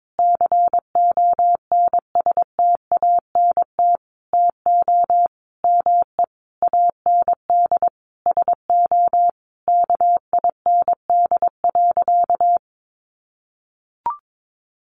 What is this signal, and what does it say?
Morse code, heard as a keyed tone, about 22 wpm, 700 Hz. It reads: CONSTANT TO ME AND SO KIND.
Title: Sentences from Top 1500 Words 22wpm